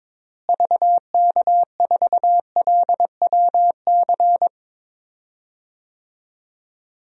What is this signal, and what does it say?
Morse code, heard as a keyed tone, about 22 wpm, 700 Hz. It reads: VK4LWC
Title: DXCC-100 Call Signs - 6 Characters - Repeated 3x - 22wpm